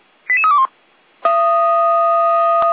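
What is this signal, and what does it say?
Five-tone selective call (ZVEI), heard as quick runs of five tones, tones of about 70 ms. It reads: 88321
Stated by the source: Five-tone ZVEI selective call received on SDR.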